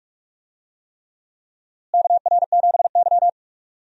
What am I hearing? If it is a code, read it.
Morse code, heard as a keyed tone, about 43 wpm, 700 Hz. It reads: KR7Y
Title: USA Call Signs - 4 Characters - Repeated 3x - 45wpm